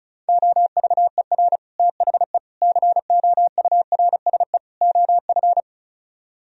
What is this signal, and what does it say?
Morse code, heard as a keyed tone, about 44 wpm, 700 Hz. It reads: OVER THE COURSE OF
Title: Speed-Racing - Sets of 4 Words 35wpm